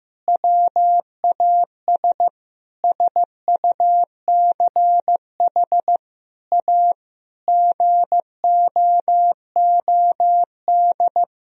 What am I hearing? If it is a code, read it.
Morse code, heard as a keyed tone, about 15 wpm, 700 Hz. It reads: WAS SUCH A GOOD
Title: Sentences from Top 1000 Words - Review Part 4 - 15wpm